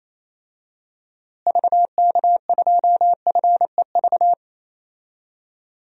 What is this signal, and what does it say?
Morse code, heard as a keyed tone, about 27 wpm, 700 Hz. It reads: VK2FEV
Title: DXCC-100 Call Signs - 6 Characters - Repeated 2x - 28wpm